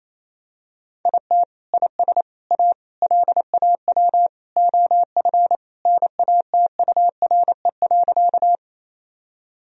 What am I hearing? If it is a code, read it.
Morse code, heard as a keyed tone, about 28 wpm, 700 Hz. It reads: IT IS A LAW OF NATURE.
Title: Sentences from Top 1000 Words - Review Part 3 - 28wpm